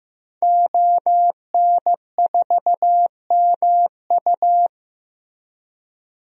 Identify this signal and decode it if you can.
Morse code, heard as a keyed tone, about 15 wpm, 700 Hz. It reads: ON4MU
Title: DXCC-100 Call Signs - 5 Characters - Repeated 3x - 15wpm